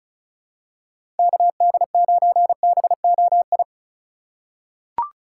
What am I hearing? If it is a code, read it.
Morse code, heard as a keyed tone, about 34 wpm, 700 Hz. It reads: KD9BOI
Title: USA Call Signs - 6 Characters - 2x Character Spacing - 35wpm